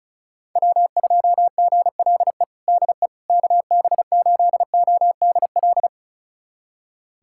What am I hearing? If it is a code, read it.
Morse code, heard as a keyed tone, about 36 wpm, 700 Hz. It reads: W2GLE DE KB8ODL
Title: USA Callsign DE Callsign 35wpm